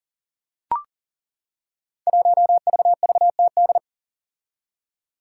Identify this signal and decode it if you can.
Morse code, heard as a keyed tone, about 41 wpm, 700 Hz. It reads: 1VVTD
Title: Five Challenging Letter-Number Combinations 40wpm